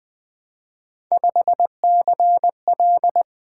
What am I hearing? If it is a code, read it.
Morse code, heard as a keyed tone, about 20 wpm, 700 Hz. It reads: 5CL
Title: Three Letter-Number Combinations - 20wpm